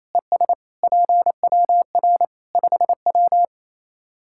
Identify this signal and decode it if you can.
Morse code, heard as a keyed tone, about 28 wpm, 700 Hz. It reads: ES PWR 5W
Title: Practice QSO - Exchange 2 - Info - 28wpm